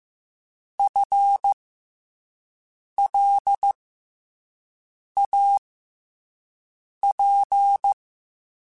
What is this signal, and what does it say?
Morse code, heard as a keyed tone, about 15 wpm, spaced as for 5 wpm, 800 Hz. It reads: FLAP
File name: FL Runs / FL Runs mp3s / FL Runs @5